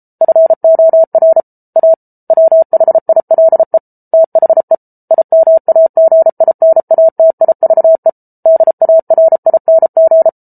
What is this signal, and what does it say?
Morse code, heard as a keyed tone, about 34 wpm, 650 Hz. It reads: FOR A WHILE THE IMAGINATIVE DARING